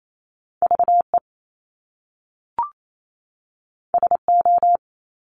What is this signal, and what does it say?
Morse code, heard as a keyed tone, about 28 wpm, 700 Hz. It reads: V E S O
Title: Top Two Letter Combinations in Words 28wpm